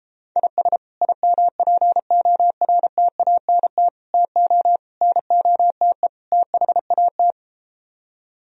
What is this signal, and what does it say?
Morse code, heard as a keyed tone, about 33 wpm, 700 Hz. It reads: IS IMPORTANT TO NOTE THAT